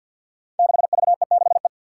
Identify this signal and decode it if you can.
Morse code, heard as a keyed tone, about 45 wpm, 700 Hz. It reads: 6VE6E
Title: Five Challenging Letter-Number Combinations 50wpm